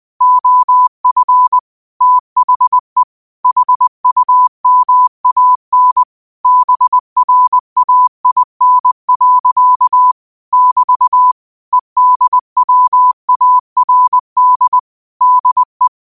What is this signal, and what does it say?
Morse code, heard as a keyed tone, about 20 wpm, 1000 Hz. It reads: OF THE HUMAN BRAIN. <BT> EDWARD DE
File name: qotdfiles / 20wpm